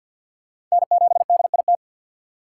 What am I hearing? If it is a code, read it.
Morse code, heard as a keyed tone, about 48 wpm, 700 Hz. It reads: N7DIT
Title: USA Call Signs - 5 Characters - 6x Character Spacing - 50wpm